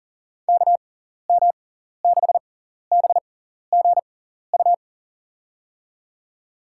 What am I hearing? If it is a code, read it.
Morse code, heard as a keyed tone, about 41 wpm, spaced as for 14 wpm, 700 Hz. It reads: KM6BGU
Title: USA Call Signs - 6 Characters - 6x Character Spacing - 40wpm